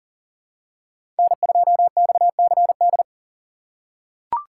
Morse code, heard as a keyed tone, about 40 wpm, 700 Hz. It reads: N2XCD
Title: USA Call Signs - 2x Character Spacing - 40wpm